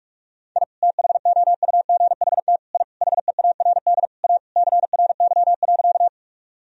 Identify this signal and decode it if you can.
Morse code, heard as a keyed tone, about 46 wpm, 700 Hz. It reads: I THOUGHT I HEARD A CRY.
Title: Sentences from Top 1000 Words - Review Part 2 - 45wpm